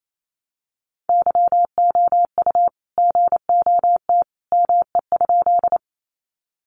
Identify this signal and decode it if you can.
Morse code, heard as a keyed tone, about 28 wpm, 700 Hz. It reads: YOU GOT ME?
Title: Sentences from Top 300 Words 28wpm